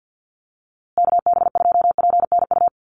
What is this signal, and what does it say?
Morse code, heard as a keyed tone, about 50 wpm, 700 Hz. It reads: KB2PNU
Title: USA Call Signs Repeated 2x 50wpm